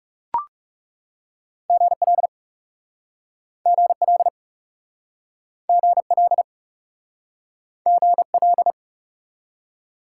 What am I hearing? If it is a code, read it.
Morse code, heard as a keyed tone, about 37 wpm, 700 Hz. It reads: GL GL GL GL
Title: Speed-Racing Top 90 QSO Elements 30wpm